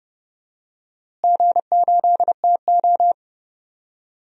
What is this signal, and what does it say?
Morse code, heard as a keyed tone, about 31 wpm, 700 Hz. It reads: G8TO
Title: DXCC-100 Call Signs - 4 Characters - Repeated 3x - 30wpm